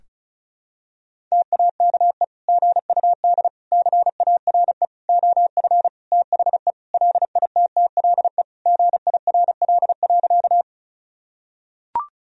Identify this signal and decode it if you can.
Morse code, heard as a keyed tone, about 36 wpm, 700 Hz. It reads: TAKE GUD CARE OF THE LITTLE GIRL.
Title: Sentences from Top 400 Words and Abbreviations 35wpm